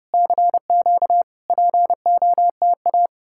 Morse code, heard as a keyed tone, about 34 wpm, 700 Hz. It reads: CQ POTA